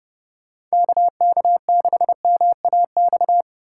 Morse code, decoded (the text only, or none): KK6MAX